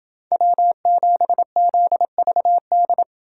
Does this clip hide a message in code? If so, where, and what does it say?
Yes: Morse code from 0.3 to 3.0 s, heard as a keyed tone, about 29 wpm, 700 Hz. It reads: W7ZVD